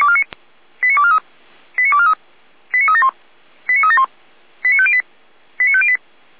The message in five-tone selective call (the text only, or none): 89243 89243 89371 89371 89508 89508